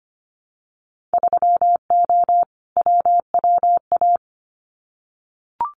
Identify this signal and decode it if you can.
Morse code, heard as a keyed tone, about 25 wpm, 700 Hz. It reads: 3O WWA
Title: Winter Field Day Exchanges Encore 25wpm